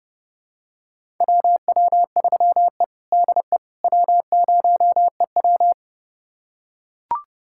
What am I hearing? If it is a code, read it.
Morse code, heard as a keyed tone, about 30 wpm, 700 Hz. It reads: WW3E DE W0EW
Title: USA Callsign DE Callsign - 4 Characters - 30wpm